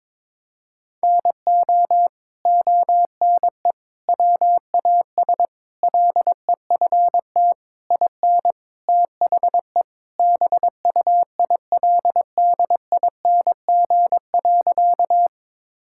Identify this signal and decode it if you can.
Morse code, heard as a keyed tone, about 22 wpm, 700 Hz. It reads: NO ONE WAS LEFT IN THE BUILDING.
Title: Sentences from Top 1000 Words - Review Part 5 - 22wpm